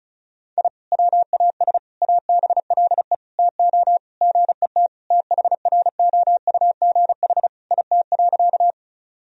Morse code, heard as a keyed tone, about 34 wpm, 700 Hz. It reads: I WAS ABLE TO GET THROUGH IT.
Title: Sentences from Top 1000 Words - Review Part 2 - 35wpm